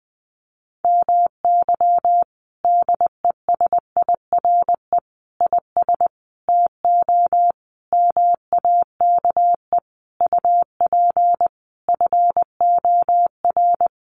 Morse code, heard as a keyed tone, about 20 wpm, 700 Hz. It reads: MY DESIRE IS TO MAKE UP FOR
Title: Sentences from Top 1100 Words 20wpm